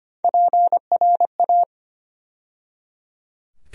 Morse code, heard as a keyed tone, about 25 wpm, 700 Hz. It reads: PRA
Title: USA Call Sign Postfixes - 3 Letters 25wpm